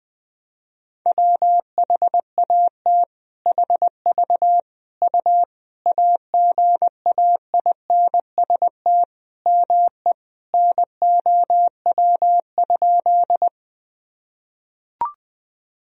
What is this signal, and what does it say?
Morse code, heard as a keyed tone, about 20 wpm, 700 Hz. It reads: WHAT HV U AGAINST ME NOW?
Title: Sentences from Top 400 Words and Abbreviations 20wpm